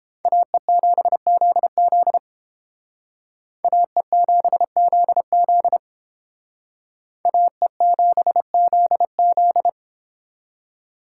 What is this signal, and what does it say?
Morse code, heard as a keyed tone, about 29 wpm, 700 Hz. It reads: AE7ZZ AE7ZZ AE7ZZ